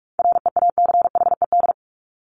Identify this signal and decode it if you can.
Morse code, heard as a keyed tone, about 44 wpm, 700 Hz. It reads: REACHED
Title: ED Suffix - Top 100 Words with Base Words 45wpm